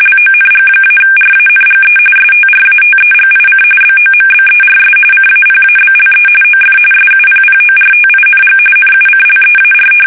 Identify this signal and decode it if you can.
RTTY, about 45.45 baud. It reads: LONG-RANGE STRATEGIC BOMBERS CAPABLE OF SUPPORTING NUCLEAR
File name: fsk850 / usnavy5